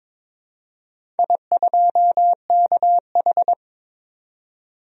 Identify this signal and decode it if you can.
Morse code, heard as a keyed tone, about 21 wpm, 690 Hz. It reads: I2KH
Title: DXCC-100 Call Signs - 4 Characters - 22wpm